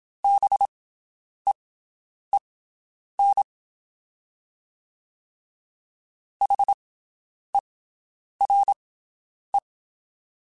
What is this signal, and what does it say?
Morse code, heard as a keyed tone, about 27 wpm, spaced as for 9 wpm, 800 Hz. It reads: BEEN HERE
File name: (spaced) / QSOs Spaced-mp3s / WD0BGP 1 @25 sp